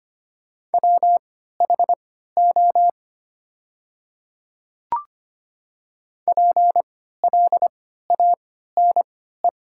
Morse code, heard as a keyed tone, about 25 wpm, spaced as for 14 wpm, 700 Hz. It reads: WHO PLANE
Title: 3x Character Spacing - Top 500 Words - 25wpm